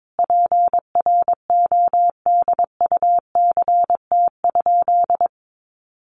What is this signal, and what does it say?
Morse code, heard as a keyed tone, about 22 wpm, 700 Hz. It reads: PRODUCT?
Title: Sentences from Top 1400 Words 22wpm